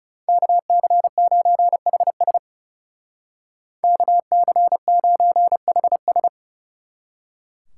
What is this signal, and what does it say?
Morse code, heard as a keyed tone, about 37 wpm, 700 Hz. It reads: KC9HS KC9HS
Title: Speed Racing - USA 2x2 Call Signs 30wpm